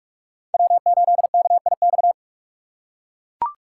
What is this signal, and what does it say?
Morse code, heard as a keyed tone, about 40 wpm, 700 Hz. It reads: W8KIX